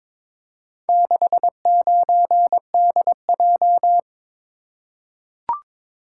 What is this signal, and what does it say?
Morse code, heard as a keyed tone, about 22 wpm, 700 Hz. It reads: 69DJ
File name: practice / 22wpm / Four Letter-Number Combinations - 22wpm